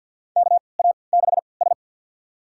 Morse code, heard as a keyed tone, about 53 wpm, spaced as for 29 wpm, 700 Hz. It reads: KA6S
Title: USA Call Signs - 4 Characters - 3x Character Spacing - 50wpm